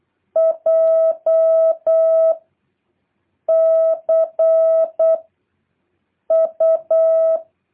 Morse code, heard as a keyed tone, about 8 wpm, 650 Hz. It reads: J C U
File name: jh8cbh / 2023cw